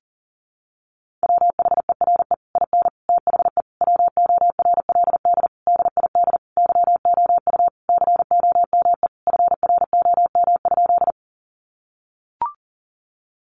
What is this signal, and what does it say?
Morse code, heard as a keyed tone, about 40 wpm, 700 Hz. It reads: WHERE IN THE WORLD DID YOU COME FROM?
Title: Sentences from Top 200 Words 40wpm